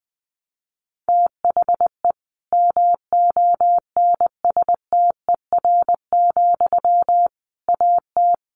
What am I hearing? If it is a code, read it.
Morse code, heard as a keyed tone, about 20 wpm, 700 Hz. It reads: THE MONSTER, AT
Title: A Sound Of Thunder 20wpm